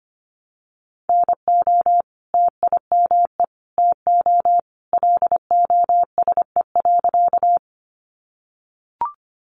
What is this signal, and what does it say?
Morse code, heard as a keyed tone, about 25 wpm, 700 Hz. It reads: NO TIME TO LOSE.
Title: Sentences from Top 1100 Words 25wpm